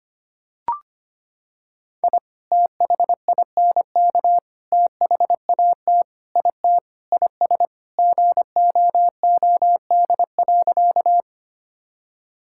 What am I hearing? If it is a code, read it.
Morse code, heard as a keyed tone, about 25 wpm, 700 Hz. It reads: I THINK THAT IT IS GOOD.
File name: practice / 25wpm / Sentences with 3-Word Phrases - 25wpm